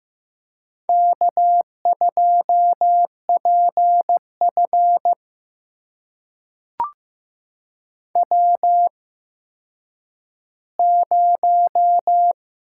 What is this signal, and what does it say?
Morse code, heard as a keyed tone, about 15 wpm, spaced as for 9 wpm, 700 Hz. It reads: K2PF W0
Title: USA Call Signs - 4 Characters - 8x Character Spacing - 15wpm